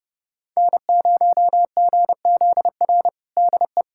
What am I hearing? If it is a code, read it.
Morse code, heard as a keyed tone, about 29 wpm, 700 Hz. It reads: N0GZR DE
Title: USA Callsign DE Callsign - 5 Characters - 30wpm